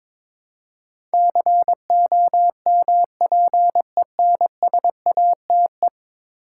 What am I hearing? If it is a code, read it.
Morse code, heard as a keyed tone, about 22 wpm, 700 Hz. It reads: COMPENSATE
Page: morse-code-ninja-practice-22wpm